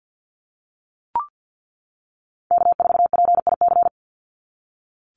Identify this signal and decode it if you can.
Morse code, heard as a keyed tone, about 48 wpm, 700 Hz. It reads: K4PIC
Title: NA SOTA Chaser Callsigns 50wpm